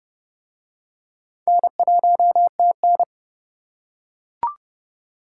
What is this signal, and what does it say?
Morse code, heard as a keyed tone, about 29 wpm, 700 Hz. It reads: N1TN